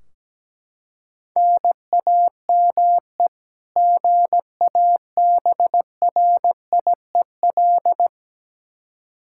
Morse code, heard as a keyed tone, about 17 wpm, 700 Hz. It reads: NAME GABRIEL
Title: K1USN SST QSO Practice 17wpm